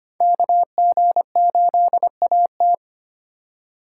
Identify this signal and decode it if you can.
Morse code, heard as a keyed tone, about 34 wpm, spaced as for 30 wpm, 700 Hz. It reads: KG8AT
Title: Speed Racing - USA 2x2 Call Signs 25wpm